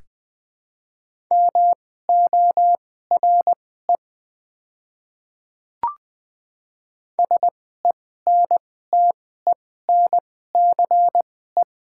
Morse code, heard as a keyed tone, about 20 wpm, spaced as for 15 wpm, 700 Hz. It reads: MORE SENTENCE
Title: 2x Character Spacing - Top 200 Words - 20wpm